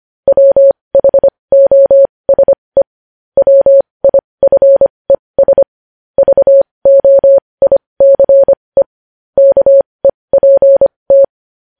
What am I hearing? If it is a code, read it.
Morse code, heard as a keyed tone, about 25 wpm, spaced as for 20 wpm, 550 Hz. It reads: WHOSE WIFES VOICE KEPT